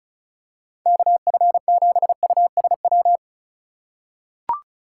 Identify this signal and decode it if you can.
Morse code, heard as a keyed tone, about 36 wpm, 700 Hz. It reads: KF7USW
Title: USA Call Signs - 6 Characters - 4x Character Spacing - 35wpm